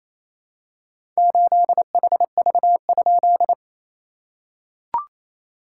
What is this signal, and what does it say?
Morse code, heard as a keyed tone, about 28 wpm, 660 Hz. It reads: 8HV?
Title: SOTA QSO Practice 28wpm